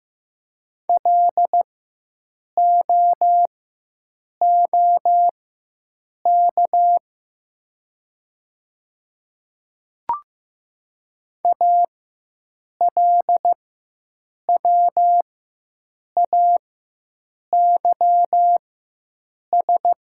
Morse code, heard as a keyed tone, about 15 wpm, spaced as for 7 wpm, 700 Hz. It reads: LOOK ALWAYS
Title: 4x Character Spacing - Top 300 Words - 15wpm